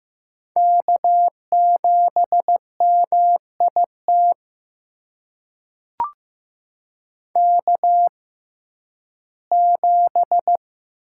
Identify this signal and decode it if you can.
Morse code, heard as a keyed tone, about 15 wpm, spaced as for 11 wpm, 700 Hz. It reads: K7MIT K7
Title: USA Call Signs - 5 Characters - 6x Character Spacing - 15wpm